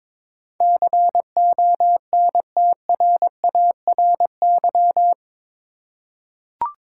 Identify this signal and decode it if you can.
Morse code, heard as a keyed tone, about 22 wpm, 700 Hz. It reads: CONTRARY
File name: practice / 22wpm / Eight Letter Words - 22wpm